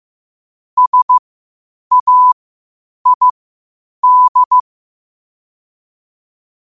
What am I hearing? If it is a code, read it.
Morse code, heard as a keyed tone, about 15 wpm, spaced as for 8 wpm, 1000 Hz. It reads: SAID
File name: qotdfiles / 8wpm